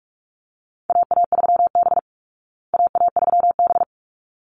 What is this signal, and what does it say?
Morse code, heard as a keyed tone, about 44 wpm, 700 Hz. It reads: AA3B AA3B
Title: Call Signs Repeated 3x - 45wpm